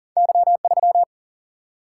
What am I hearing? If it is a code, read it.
Morse code, heard as a keyed tone, about 40 wpm, 700 Hz. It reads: Y3
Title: Single-Letter Single-Number 40wpm